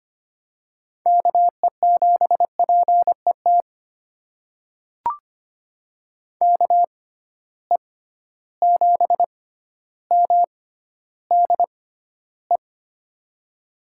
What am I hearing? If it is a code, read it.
Morse code, heard as a keyed tone, about 25 wpm, spaced as for 13 wpm, 700 Hz. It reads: KE7PET KE7MDE
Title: USA Call Signs - 6 Characters - 6x Character Spacing - 25wpm